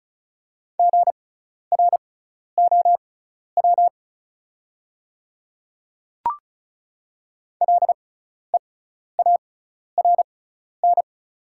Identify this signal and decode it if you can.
Morse code, heard as a keyed tone, about 36 wpm, spaced as for 12 wpm, 690 Hz. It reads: GROW LEARN
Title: 6x Character Spacing - Top 500 Words - 35wpm